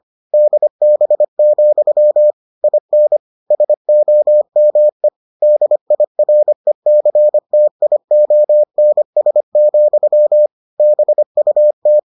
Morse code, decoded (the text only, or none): DB, IN SOME DIRECTIONS, BUT